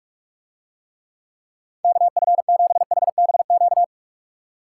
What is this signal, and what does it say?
Morse code, heard as a keyed tone, about 44 wpm, 700 Hz. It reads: KF7HBQ